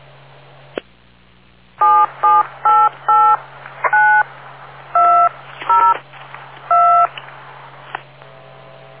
DTMF, heard as telephone touch tones, tones of about 270 ms. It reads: 77889202